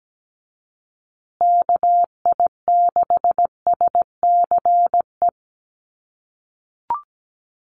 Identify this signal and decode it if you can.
Morse code, heard as a keyed tone, about 17 wpm, 700 Hz. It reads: KI6SCE
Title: USA Call Signs Repeated 2x 17wpm